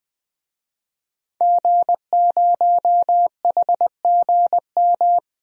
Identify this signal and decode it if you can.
Morse code, heard as a keyed tone, about 20 wpm, 700 Hz. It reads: G0HGM
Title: DXCC-100 Call Signs - 5 Characters - 20wpm